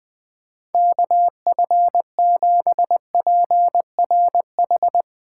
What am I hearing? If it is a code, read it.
Morse code, heard as a keyed tone, about 20 wpm, 700 Hz. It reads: KF7PRH
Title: USA Call Signs - 6 Characters - 8x Character Spacing - 20wpm